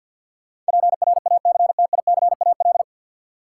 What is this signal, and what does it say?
Morse code, heard as a keyed tone, about 53 wpm, 700 Hz. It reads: PRACTICAL
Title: Sentences from Top 1400 Words - New Word Drill - 50wpm